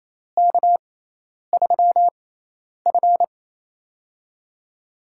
Morse code, heard as a keyed tone, about 28 wpm, spaced as for 10 wpm, 700 Hz. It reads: K3F
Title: USA Call Signs - 6x Character Spacing - 28wpm